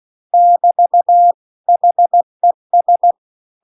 Morse code, heard as a keyed tone, about 16 wpm, 700 Hz. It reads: <BT> HES
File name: CW-The-Big-Twoheaded-River-2-16WPM